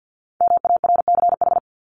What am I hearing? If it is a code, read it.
Morse code, heard as a keyed tone, about 50 wpm, 700 Hz. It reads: MARCH
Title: Top 1501-1750 Words 50wpm